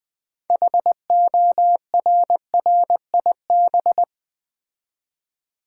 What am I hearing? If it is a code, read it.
Morse code, heard as a keyed tone, about 20 wpm, 700 Hz. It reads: HORRIB
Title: Build A Word - 2 Letters At A Time - 8-Letter Words - 20wpm